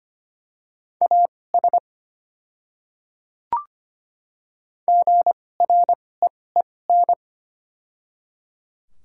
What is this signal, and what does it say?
Morse code, heard as a keyed tone, about 25 wpm, spaced as for 18 wpm, 700 Hz. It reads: AS GREEN
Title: 2x Character Spacing - Top 500 Words - 25wpm